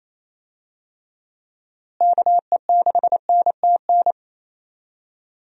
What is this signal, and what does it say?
Morse code, heard as a keyed tone, about 28 wpm, 700 Hz. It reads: KE6NTN
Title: USA Call Signs - 6 Characters - Repeated 3x - 28wpm